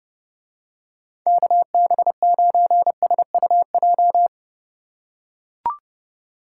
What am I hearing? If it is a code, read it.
Morse code, heard as a keyed tone, about 30 wpm, 700 Hz. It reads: KB9SUJ